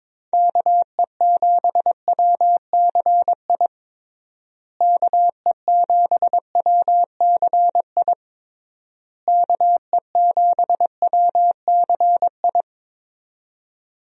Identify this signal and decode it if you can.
Morse code, heard as a keyed tone, about 22 wpm, 700 Hz. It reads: KE7WCI KE7WCI KE7WCI
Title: USA Call Signs - 6 Characters - Repeated 3x - 22wpm